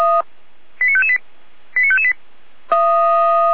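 Five-tone selective call (ZVEI), heard as quick runs of five tones, tones of about 70 ms. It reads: 89558 89558
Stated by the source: ZVEI five-tone selective call received with an SDR.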